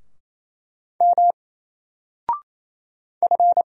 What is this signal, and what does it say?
Morse code, heard as a keyed tone, about 27 wpm, 700 Hz. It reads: M F
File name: Single-Letter-Number-28wpm